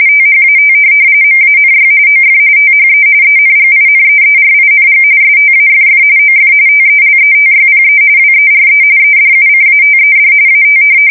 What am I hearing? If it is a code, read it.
RTTY, about 45.45 baud. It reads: THAT MANY OF THE ENGINEERING STAFF OF TELETYPE WERE EDUCATED AT
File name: fsk170 / teletype-corp